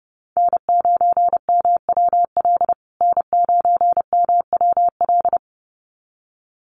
Morse code, heard as a keyed tone, about 30 wpm, 700 Hz. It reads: N9MWL N9MWL
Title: USA Call Signs Repeated 2x 30wpm